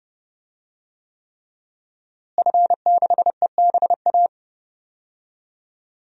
Morse code, heard as a keyed tone, about 30 wpm, 700 Hz. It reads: F6EBA
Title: DXCC-100 Call Signs - 5 Characters - Repeated 3x - 30wpm